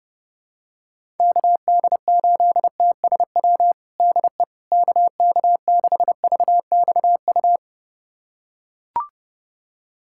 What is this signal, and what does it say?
Morse code, heard as a keyed tone, about 30 wpm, 700 Hz. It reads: KD8TSW DE KK6VXU
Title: USA Callsign DE Callsign - 6 Characters - 30wpm